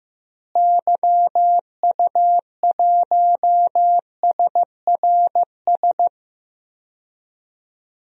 Morse code, heard as a keyed tone, about 15 wpm, 700 Hz. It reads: YU1SRS
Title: DXCC-100 Call Signs - 6 Characters - Repeated 2x - 15wpm